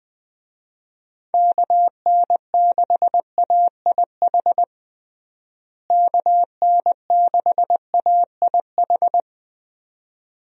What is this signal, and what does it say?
Morse code, heard as a keyed tone, about 20 wpm, 700 Hz. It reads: KN6AIH KN6AIH
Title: USA Call Signs - 6 Characters - Repeated 2x - 20wpm